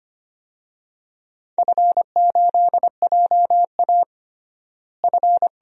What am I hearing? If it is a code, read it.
Morse code, heard as a keyed tone, about 25 wpm, 700 Hz. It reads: F8JA F
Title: DXCC-100 Call Signs - 4 Characters - Repeated 3x - 25wpm